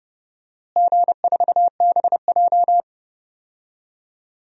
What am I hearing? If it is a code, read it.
Morse code, heard as a keyed tone, about 30 wpm, 700 Hz. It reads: G4BJ
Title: DXCC-100 Call Signs - 4 Characters - Repeated 2x - 30wpm